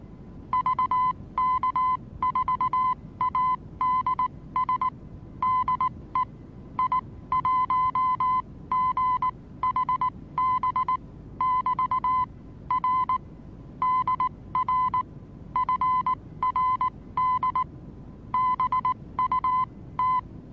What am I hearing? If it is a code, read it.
Morse code, heard as a keyed tone, about 19 wpm, spaced as for 16 wpm, 1020 Hz. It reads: VK4ADS DE I1GHB <BT> R DR FRD BUT
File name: repos / Morsekurs / DJ4UF Morsen26